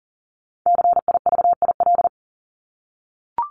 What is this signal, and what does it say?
Morse code, heard as a keyed tone, about 40 wpm, 700 Hz. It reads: CIVIL
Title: Top 1751-2000 Words 40wpm